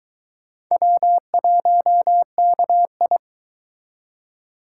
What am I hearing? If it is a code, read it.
Morse code, heard as a keyed tone, about 22 wpm, 700 Hz. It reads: W1KI